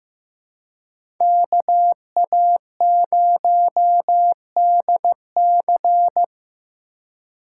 Morse code, heard as a keyed tone, about 16 wpm, 700 Hz. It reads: KA0DC
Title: USA Call Signs - 3x Character Spacing - 15wpm